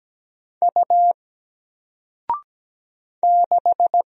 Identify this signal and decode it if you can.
Morse code, heard as a keyed tone, about 17 wpm, 700 Hz. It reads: U 6
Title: Single Letter-Number Alt 17wpm